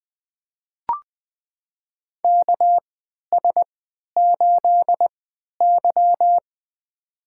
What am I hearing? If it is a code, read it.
Morse code, heard as a keyed tone, about 20 wpm, spaced as for 11 wpm, 700 Hz. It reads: KS8Y
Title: USA Call Signs - 4 Characters - 3x Character Spacing - 20wpm